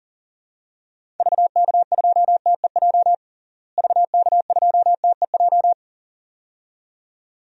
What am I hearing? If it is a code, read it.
Morse code, heard as a keyed tone, about 40 wpm, 700 Hz. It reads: VK2TEJ VK2TEJ
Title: DXCC-100 Call Signs - 6 Characters - Repeated 2x - 40wpm